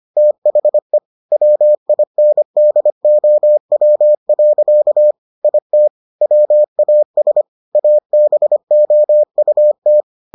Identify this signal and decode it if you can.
Morse code, heard as a keyed tone, about 25 wpm, 600 Hz. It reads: THE WINDOW. IT WAS ABOUT